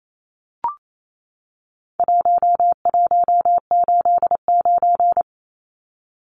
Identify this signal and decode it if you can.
Morse code, heard as a keyed tone, about 28 wpm, 700 Hz. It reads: 1189
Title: Numbers - 4-Digit 28wpm